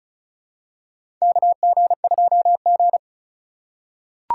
Morse code, heard as a keyed tone, about 36 wpm, 700 Hz. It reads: KG2G
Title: USA Call Signs - 4x Character Spacing - 35wpm